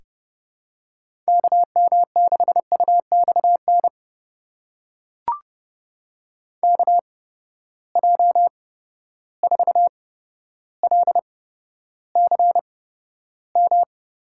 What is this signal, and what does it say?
Morse code, heard as a keyed tone, about 30 wpm, spaced as for 13 wpm, 700 Hz. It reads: KM6UXN KJ4LCM
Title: USA Call Signs - 6 Characters - 8x Character Spacing - 30wpm